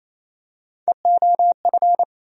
Morse code, heard as a keyed tone, about 28 wpm, 700 Hz. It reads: EOF
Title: USA Call Sign Postfixes - 3 Letters 28wpm